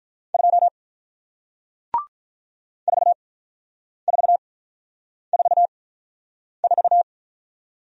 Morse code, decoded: 2 4444

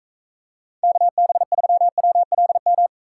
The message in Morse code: KB3WLM

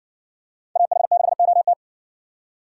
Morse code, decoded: AH6OT

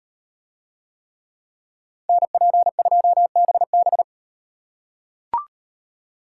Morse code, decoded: NP2BB

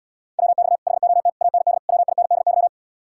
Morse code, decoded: WX HOT AND WINDY

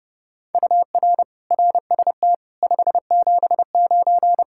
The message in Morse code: UR RST 579

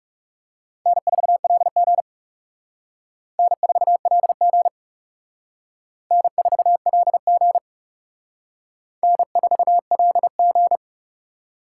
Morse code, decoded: N4LG N4LG N4LG N4LG

